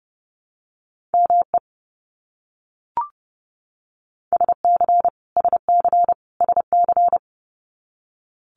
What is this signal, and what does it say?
Morse code, heard as a keyed tone, about 30 wpm, 700 Hz. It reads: ME SC SC SC
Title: US States Repeated 3x 30wpm